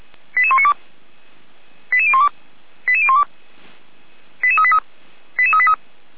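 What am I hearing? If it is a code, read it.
Five-tone selective call (ZVEI), heard as quick runs of five tones, tones of about 70 ms. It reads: 88192 88012 88012 88383 88383
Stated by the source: ZVEI five-tone selective call received by SDR, hosted by Feuerwehr Orlamünde fire brigade.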